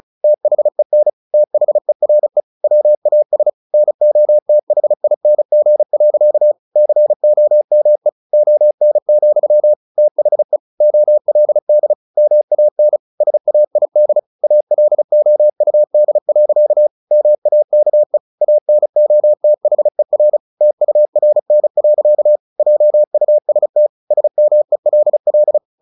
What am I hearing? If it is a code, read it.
Morse code, heard as a keyed tone, about 34 wpm, 600 Hz. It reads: THEN THERE WAS NOTHING. COME ON, THE OLD MAN SAID ALOUD. MAKE ANOTHER TURN. JUST SMELL
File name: the old man and the sea / the old man and the sea 1 35wpm